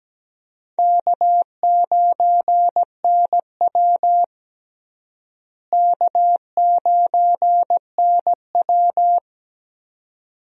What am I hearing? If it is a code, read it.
Morse code, heard as a keyed tone, about 17 wpm, 700 Hz. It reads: K9NW K9NW
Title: USA Call Signs - 4 Characters - Repeated 2x - 17wpm